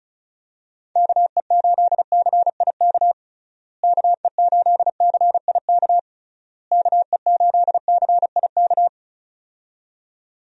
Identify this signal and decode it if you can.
Morse code, heard as a keyed tone, about 34 wpm, 700 Hz. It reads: KE8CIK KE8CIK KE8CIK
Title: USA Call Signs - 6 Characters - Repeated 3x - 35wpm